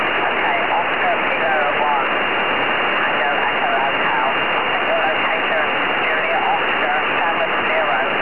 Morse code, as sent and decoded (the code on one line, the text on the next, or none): none